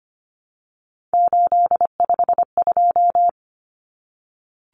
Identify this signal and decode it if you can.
Morse code, heard as a keyed tone, about 25 wpm, 700 Hz. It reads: 852
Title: Common Cut Numbers - 3-Digit 25wpm